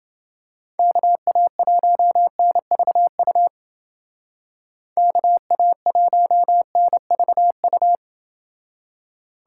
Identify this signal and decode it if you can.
Morse code, heard as a keyed tone, about 29 wpm, 700 Hz. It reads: KA1NVU KA1NVU